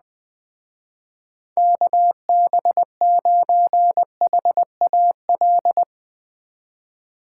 Morse code, heard as a keyed tone, about 21 wpm, 700 Hz. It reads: KB9HAL